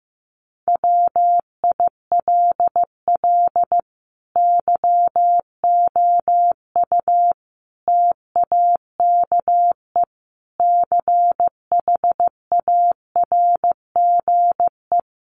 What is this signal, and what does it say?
Morse code, heard as a keyed tone, about 14 wpm, 700 Hz. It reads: WILL YOU TAKE CHARGE
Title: Sentences from Top 900 Words 15wpm